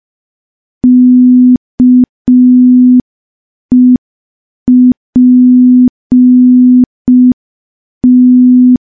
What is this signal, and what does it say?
Morse code, heard as a keyed tone, about 5 wpm, 250 Hz. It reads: KEPT
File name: morse / morse-training-05WPM